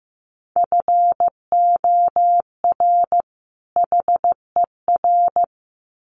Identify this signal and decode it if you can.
Morse code, heard as a keyed tone, about 15 wpm, 700 Hz. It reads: FOR HER
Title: Sets of 2 Words 15wpm